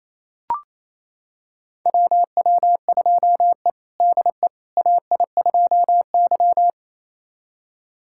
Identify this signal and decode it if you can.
Morse code, heard as a keyed tone, about 28 wpm, 700 Hz. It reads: WW2E DE AI2Y
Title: USA Callsign DE Callsign - 4 Characters - 28wpm